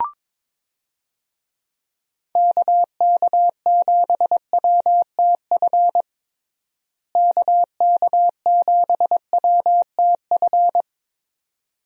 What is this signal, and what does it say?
Morse code, heard as a keyed tone, about 22 wpm, 700 Hz. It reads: KK7WTF KK7WTF
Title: USA Call Signs - 6 Characters - Repeated 2x - 22wpm